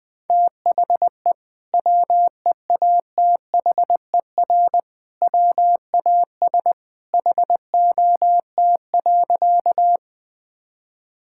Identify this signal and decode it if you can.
Morse code, heard as a keyed tone, about 20 wpm, 700 Hz. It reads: THE WEATHER WAS HOT.
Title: Sentences from Top 1000 Words - Review Part 5 - 20wpm